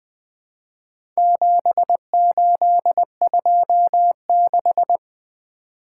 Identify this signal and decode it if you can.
Morse code, heard as a keyed tone, about 20 wpm, 700 Hz. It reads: 7826
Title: Four Letter-Number Combinations - 20wpm